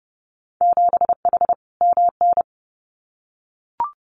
Morse code, heard as a keyed tone, about 30 wpm, 700 Hz. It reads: 7H MN
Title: Winter Field Day Exchanges Encore 30wpm